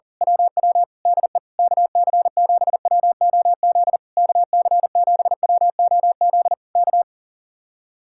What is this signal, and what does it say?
Morse code, heard as a keyed tone, about 38 wpm, 700 Hz. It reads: WW DE KC7WOZ KC7WOZ K
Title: Speed-Racing Common QSO Phrases Encore - 40wpm